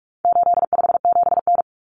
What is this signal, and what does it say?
Morse code, heard as a keyed tone, about 44 wpm, 700 Hz. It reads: 857N
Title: Common Cut Numbers - 4-Digit 45wpm